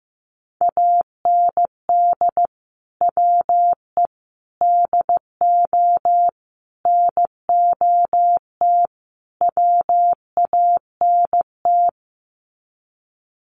Morse code, heard as a keyed tone, about 15 wpm, 700 Hz. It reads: AND WE DO NOT WANT
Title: Sets of 5 Words Encore2 15wpm